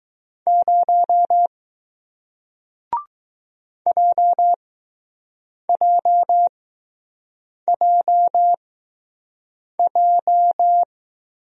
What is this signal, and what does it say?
Morse code, heard as a keyed tone, about 20 wpm, spaced as for 6 wpm, 700 Hz. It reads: 0 JJJJ